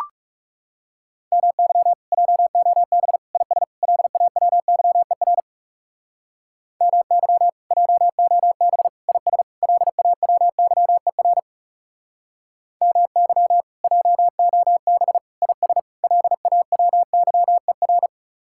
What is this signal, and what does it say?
Morse code, heard as a keyed tone, about 37 wpm, 700 Hz. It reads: MY JOB IS LAWYER MY JOB IS LAWYER MY JOB IS LAWYER
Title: Speed-Racing Common QSO Phrases - 30wpm